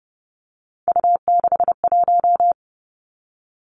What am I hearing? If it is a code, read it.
Morse code, heard as a keyed tone, about 30 wpm, 700 Hz. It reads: U61
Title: Three Challenging Letter-Number Combinations 30wpm